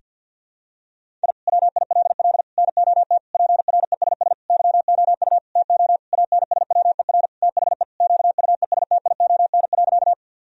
Morse code, heard as a keyed tone, about 49 wpm, 700 Hz. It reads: I WILL NOT PRESS YOU TO ANSWER THE QUESTION.
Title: Sentences from Top 1000 Words - Review Part 1 - 50wpm